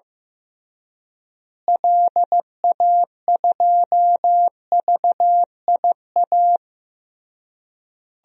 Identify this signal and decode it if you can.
Morse code, heard as a keyed tone, about 15 wpm, 700 Hz. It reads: LA2VIA
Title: DXCC-100 Call Signs - 6 Characters - Repeated 3x - 15wpm